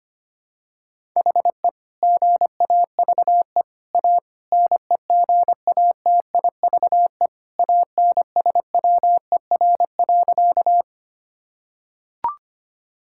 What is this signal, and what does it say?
Morse code, heard as a keyed tone, about 25 wpm, 700 Hz. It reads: HE GAVE A NEGATIVE ANSWER.
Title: Sentences from Top 1000 Words - Review Part 6 - 25wpm